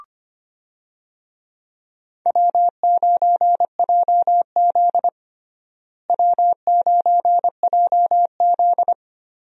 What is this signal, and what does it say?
Morse code, heard as a keyed tone, about 25 wpm, 700 Hz. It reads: W9JZ W9JZ
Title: USA Call Signs - 4 Characters - Repeated 2x - 25wpm